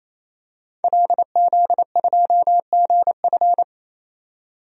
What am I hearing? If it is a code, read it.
Morse code, heard as a keyed tone, about 28 wpm, 700 Hz. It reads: LZ2GF
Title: DXCC-100 Call Signs - 5 Characters - 28wpm